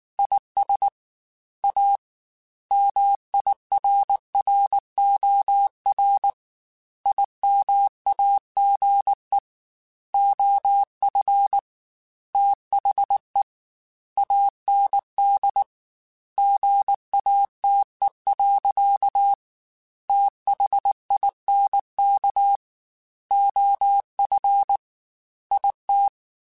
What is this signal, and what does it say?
Morse code, heard as a keyed tone, about 19 wpm, 800 Hz. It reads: IS A MIRROR IMAGE OF THE AND GATE. THINK OF IT
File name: Technical Subjects / Technical Subjects-mp3s / OR Gate @18